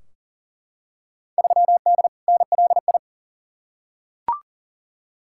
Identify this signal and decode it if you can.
Morse code, heard as a keyed tone, about 40 wpm, 700 Hz. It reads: 3D NLI
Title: Field Day Exchanges Encore - 40wpm